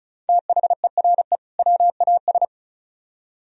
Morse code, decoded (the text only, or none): THERE WAS